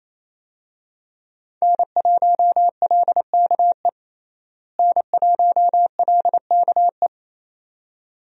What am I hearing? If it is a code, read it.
Morse code, heard as a keyed tone, about 28 wpm, 700 Hz. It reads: N1LKE N1LKE
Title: USA Call Signs - 5 Characters - Repeated 2x - 28wpm